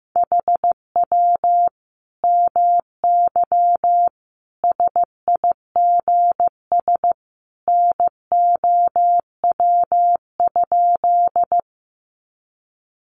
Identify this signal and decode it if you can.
Morse code, heard as a keyed tone, about 15 wpm, 700 Hz. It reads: HW MY SIGS NOW?
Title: Common QSO Phrases 15wpm